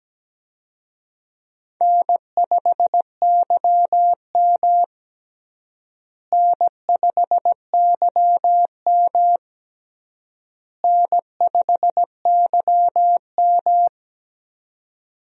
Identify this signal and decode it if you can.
Morse code, heard as a keyed tone, about 17 wpm, 700 Hz. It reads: N5YM N5YM N5YM
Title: USA Call Signs - 4 Characters - Repeated 3x - 17wpm